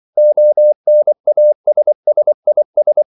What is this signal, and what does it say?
Morse code, heard as a keyed tone, about 24 wpm, 600 Hz. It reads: ONASSIS
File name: CW / Practice / EnglishQuotations0324